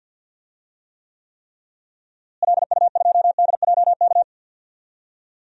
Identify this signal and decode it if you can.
Morse code, heard as a keyed tone, about 50 wpm, 700 Hz. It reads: LU2DJK
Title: DXCC-100 Call Signs - 6 Characters - Repeated 3x - 50wpm